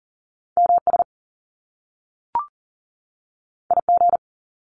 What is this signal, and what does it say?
Morse code, heard as a keyed tone, about 41 wpm, 700 Hz. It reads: MS IG
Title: Top Two Letter Combinations in Words 40wpm